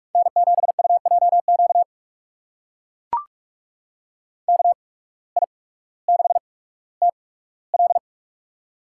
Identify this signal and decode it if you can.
Morse code, heard as a keyed tone, about 44 wpm, spaced as for 19 wpm, 690 Hz. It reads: N7UJQ KI6TL
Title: USA Call Signs - 5 Characters - 8x Character Spacing - 45wpm